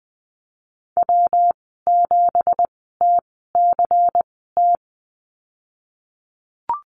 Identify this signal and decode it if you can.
Morse code, heard as a keyed tone, about 20 wpm, spaced as for 15 wpm, 700 Hz. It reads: W7TCT
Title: 193 Lesson 31g - Letter v - Call Signs 20wpm